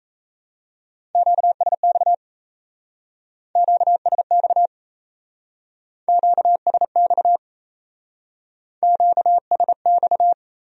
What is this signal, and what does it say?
Morse code, heard as a keyed tone, about 34 wpm, 700 Hz. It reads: QSX QSX QSX QSX
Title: Speed-Racing Q-Codes - 28wpm